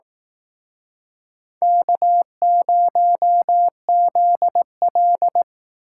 Morse code, heard as a keyed tone, about 18 wpm, 700 Hz. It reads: K0ZL
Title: Speed Racing - USA 1x2 Call Signs 15wpm